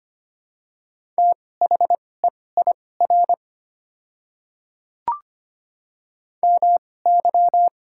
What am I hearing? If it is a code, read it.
Morse code, heard as a keyed tone, about 25 wpm, spaced as for 18 wpm, 690 Hz. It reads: THEIR MY
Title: 2x Character Spacing - Top 100 Words - 25wpm